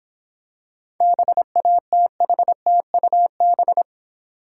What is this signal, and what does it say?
Morse code, heard as a keyed tone, about 26 wpm, 700 Hz. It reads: BATHTUB